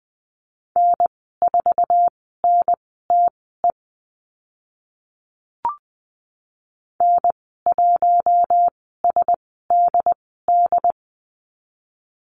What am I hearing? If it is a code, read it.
Morse code, heard as a keyed tone, about 20 wpm, spaced as for 14 wpm, 700 Hz. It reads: N4NTE N1SDD
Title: 069 Lesson 12g - Letter d - Call Signs 20wpm